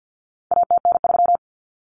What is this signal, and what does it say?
Morse code, heard as a keyed tone, about 49 wpm, 700 Hz. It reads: ATN3
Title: Common Cut Numbers - 4-Digit 50wpm